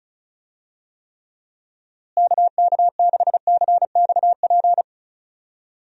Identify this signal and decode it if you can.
Morse code, heard as a keyed tone, about 36 wpm, 700 Hz. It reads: KK6CXP
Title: USA Call Signs - 6 Characters - Repeated 2x - 35wpm